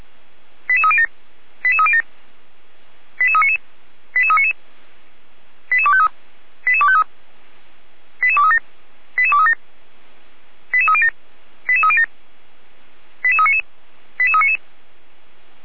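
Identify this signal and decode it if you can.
Five-tone selective call (ZVEI), heard as quick runs of five tones, tones of about 70 ms. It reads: 88307 88307 88399 88399 88253 88253 88237 88237 88307 88307 88399 88399
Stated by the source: ZVEI five-tone selective call received on an SDR.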